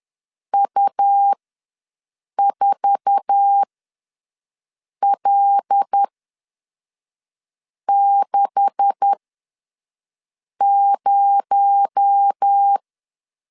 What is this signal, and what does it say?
Morse code, heard as a keyed tone, about 10 wpm, spaced as for 5 wpm, 800 Hz. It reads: U4L60
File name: LEZIONI / 9 286 ws 10wpm 4032